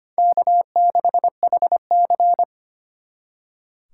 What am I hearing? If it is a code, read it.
Morse code, heard as a keyed tone, about 32 wpm, 700 Hz. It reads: K6HC